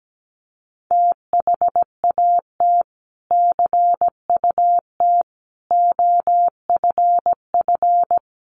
Morse code, heard as a keyed tone, about 17 wpm, 700 Hz. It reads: THAT CUT OFF
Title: Sentences from Top 1100 Words 17wpm